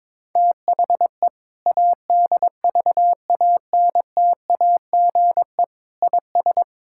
Morse code, heard as a keyed tone, about 22 wpm, 700 Hz. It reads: THE ADVANTAGE IS
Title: Sentences from Top 1000 Words - Review Part 6 - 22wpm